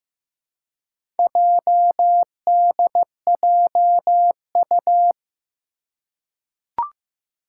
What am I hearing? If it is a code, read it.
Morse code, heard as a keyed tone, about 15 wpm, 700 Hz. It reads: JDJU
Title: Four Challenging Letter-Number Combinations 15wpm